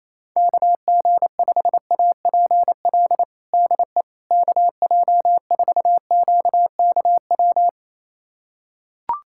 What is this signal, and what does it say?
Morse code, heard as a keyed tone, about 28 wpm, 700 Hz. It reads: KG5APL DE KJ4QKW
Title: USA Callsign DE Callsign 28wpm